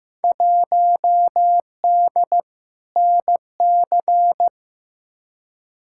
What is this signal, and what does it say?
Morse code, heard as a keyed tone, about 15 wpm, 700 Hz. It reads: 1D NC
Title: Field Day Exchanges - 15wpm